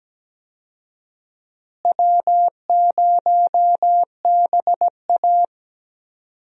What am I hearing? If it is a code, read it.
Morse code, heard as a keyed tone, about 17 wpm, 700 Hz. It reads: W0BA
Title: USA Call Signs - 4 Characters - Repeated 2x - 17wpm